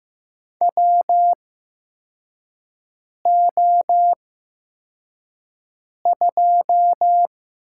Morse code, heard as a keyed tone, about 15 wpm, spaced as for 4 wpm, 700 Hz. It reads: WO2
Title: USA Call Signs - 8x Character Spacing - 15wpm